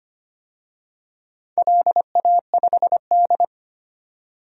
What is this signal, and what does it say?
Morse code, heard as a keyed tone, about 25 wpm, 700 Hz. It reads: LA5D
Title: DXCC-100 Call Signs - 4 Characters - 25wpm